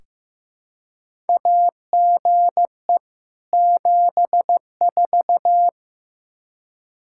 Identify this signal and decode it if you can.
Morse code, heard as a keyed tone, about 15 wpm, 700 Hz. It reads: AGE 74
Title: Practice QSO - Exchange 3 - More Info - 15wpm